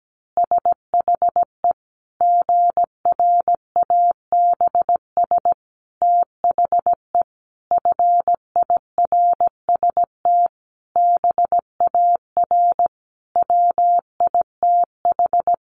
Morse code, heard as a keyed tone, about 17 wpm, 700 Hz. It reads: SHE GRABS THE FIRST BAR WITH
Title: Easy Phrases 17wpm